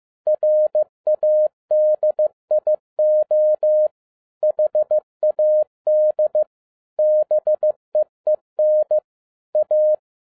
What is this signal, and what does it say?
Morse code, heard as a keyed tone, about 15 wpm, 610 Hz. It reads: RADIO HAD BEEN A